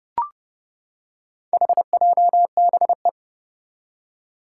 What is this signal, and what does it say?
Morse code, heard as a keyed tone, about 30 wpm, 700 Hz. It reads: HJBE
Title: Four Challenging Letter-Number Combinations 30wpm